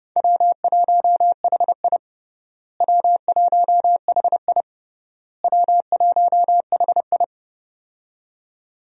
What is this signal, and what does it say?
Morse code, heard as a keyed tone, about 30 wpm, 700 Hz. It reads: W1HI W1HI W1HI
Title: USA Call Signs - 4 Characters - Repeated 3x - 30wpm